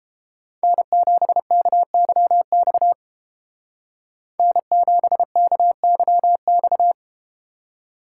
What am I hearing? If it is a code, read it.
Morse code, heard as a keyed tone, about 32 wpm, 700 Hz. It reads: N7KYX N7KYX